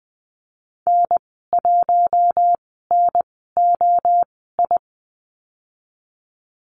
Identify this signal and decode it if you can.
Morse code, heard as a keyed tone, about 20 wpm, spaced as for 14 wpm, 700 Hz. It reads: N1NOI
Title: 042 Lesson 8g - Number 1 - Call Signs 20wpm